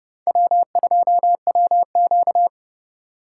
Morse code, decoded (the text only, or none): W2WQ